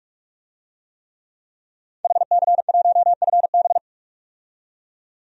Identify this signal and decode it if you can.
Morse code, heard as a keyed tone, about 42 wpm, 700 Hz. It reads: HC1FB
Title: DXCC-100 Call Signs - 5 Characters - 45wpm